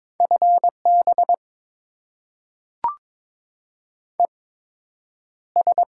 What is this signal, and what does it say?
Morse code, heard as a keyed tone, about 22 wpm, spaced as for 10 wpm, 700 Hz. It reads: FB ES
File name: Top 20 QSO Elements - 8x Character Spacing - 22wpm